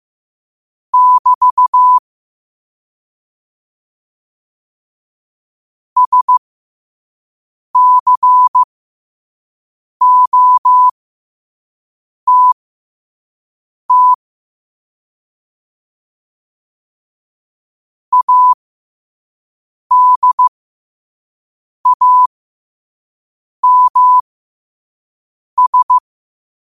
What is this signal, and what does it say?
Morse code, heard as a keyed tone, about 15 wpm, spaced as for 5 wpm, 1000 Hz. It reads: <BT> SCOTT ADAMS
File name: qotdfiles / 5wpm